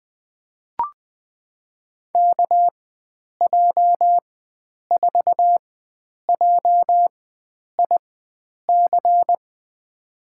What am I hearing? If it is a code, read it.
Morse code, heard as a keyed tone, about 20 wpm, spaced as for 9 wpm, 700 Hz. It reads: KJ4JIC
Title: USA Call Signs - 6 Characters - 4x Character Spacing - 20wpm